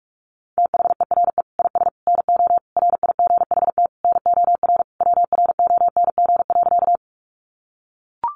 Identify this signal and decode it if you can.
Morse code, heard as a keyed tone, about 44 wpm, 700 Hz. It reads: THERE IS NO RIGHT NOR WRONG.
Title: Sentences from Top 1500 Words 45wpm